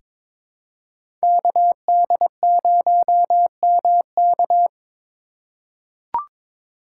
Morse code, heard as a keyed tone, about 21 wpm, 700 Hz. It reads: KD0MK
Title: USA Call Signs - 5 Characters - 2x Character Spacing - 22wpm